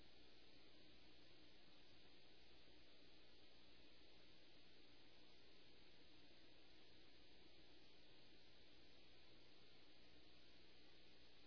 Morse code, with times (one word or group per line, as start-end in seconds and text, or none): none